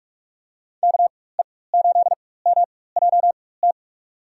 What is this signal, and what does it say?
Morse code, heard as a keyed tone, about 46 wpm, spaced as for 21 wpm, 700 Hz. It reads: KE8MJT
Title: USA Call Signs - 6 Characters - 4x Character Spacing - 45wpm